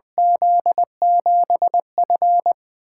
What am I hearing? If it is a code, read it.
Morse code, heard as a keyed tone, about 20 wpm, 700 Hz. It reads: Z7F
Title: USA Call Signs - 4 Characters - 8x Character Spacing - 20wpm